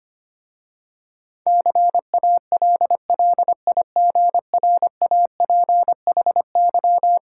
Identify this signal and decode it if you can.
Morse code, heard as a keyed tone, about 25 wpm, 700 Hz. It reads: CALLIGRAPHY
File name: practice / 25wpm / Difficult Words with Hints - 25wpm